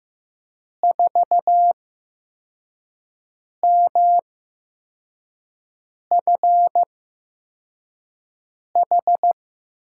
Morse code, heard as a keyed tone, about 15 wpm, spaced as for 4 wpm, 700 Hz. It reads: 4MFH
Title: USA Call Signs - 8x Character Spacing - 15wpm